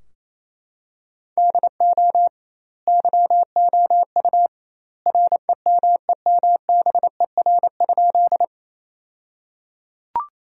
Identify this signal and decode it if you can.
Morse code, heard as a keyed tone, about 28 wpm, 700 Hz. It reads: DO YOU REMEMBER?
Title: 2x Word-Spacing - Sentences from Top 400 Words - 28wpm